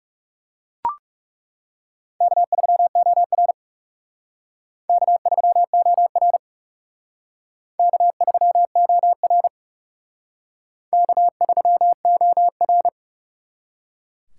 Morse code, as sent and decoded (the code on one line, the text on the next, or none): -.- ...-- --- .-. / -.- ...-- --- .-. / -.- ...-- --- .-. / -.- ...-- --- .-.
K3OR K3OR K3OR K3OR